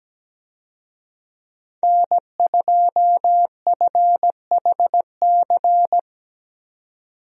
Morse code, heard as a keyed tone, about 17 wpm, 700 Hz. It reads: N2FHC
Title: USA Call Signs - 5 Characters - 17wpm